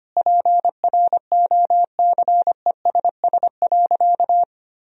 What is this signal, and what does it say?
Morse code, heard as a keyed tone, about 25 wpm, 700 Hz. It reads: PROCESS.
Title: Sentences from Top 1000 Words - Review Part 3 - 25wpm